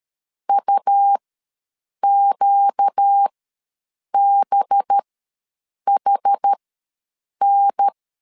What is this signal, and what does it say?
Morse code, heard as a keyed tone, about 13 wpm, spaced as for 7 wpm, 800 Hz. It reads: UQBHN
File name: LEZIONI / 5 LBVC / 5 LB ws 12wpm 3034